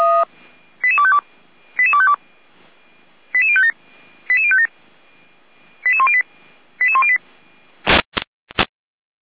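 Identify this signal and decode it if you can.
Five-tone selective call (ZVEI), heard as quick runs of five tones, tones of about 70 ms. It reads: 88252 88252 88057 88057 88108 88108
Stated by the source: Five-tone ZVEI selective call received on SDR.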